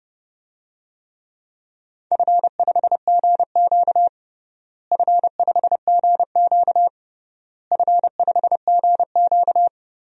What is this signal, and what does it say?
Morse code, heard as a keyed tone, about 31 wpm, 700 Hz. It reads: F5GQ F5GQ F5GQ
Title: DXCC-100 Call Signs - 4 Characters - Repeated 3x - 30wpm